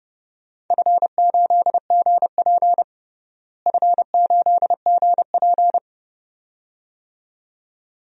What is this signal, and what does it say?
Morse code, heard as a keyed tone, about 30 wpm, 700 Hz. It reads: F8GP F8GP
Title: DXCC-100 Call Signs - 4 Characters - Repeated 2x - 30wpm